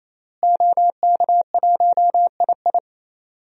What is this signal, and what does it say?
Morse code, heard as a keyed tone, about 27 wpm, 700 Hz. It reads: OK1II
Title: DXCC-100 Call Signs - 5 Characters - Repeated 3x - 28wpm